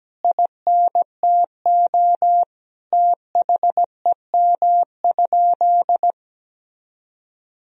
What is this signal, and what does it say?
Morse code, heard as a keyed tone, about 17 wpm, 700 Hz. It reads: INTO THEM?
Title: Sentences from Top 1000 Words - Review Part 1 - 17wpm